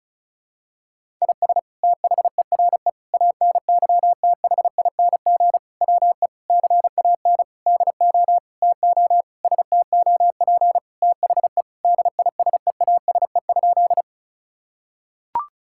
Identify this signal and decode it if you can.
Morse code, heard as a keyed tone, about 34 wpm, 700 Hz. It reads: IS THERE ANYTHING WE CAN DO TO STOP THE DISEASE?
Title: Sentences from Top 1000 Words - Review Part 6 - 35wpm